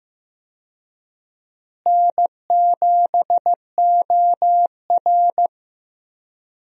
Morse code, heard as a keyed tone, about 15 wpm, 700 Hz. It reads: N7OR
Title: USA Call Signs - 4 Characters - Repeated 3x - 15wpm